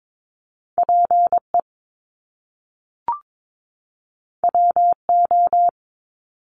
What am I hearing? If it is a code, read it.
Morse code, heard as a keyed tone, about 22 wpm, 700 Hz. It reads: PE WO
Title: Top Two Letter Combinations in Words 22wpm